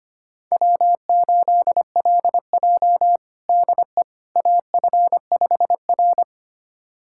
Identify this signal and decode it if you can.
Morse code, heard as a keyed tone, about 25 wpm, 700 Hz. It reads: W8LJ DE AF5R
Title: USA Callsign DE Callsign - 4 Characters - 25wpm